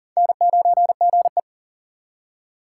Morse code, heard as a keyed tone, about 42 wpm, 700 Hz. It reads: N9GE
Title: USA Call Signs - 4 Characters - 6x Character Spacing - 40wpm